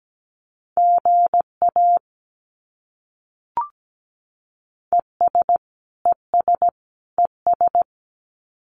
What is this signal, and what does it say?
Morse code, heard as a keyed tone, about 17 wpm, 700 Hz. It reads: GA ES ES ES
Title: Abbreviations Repeated 3x - 17wpm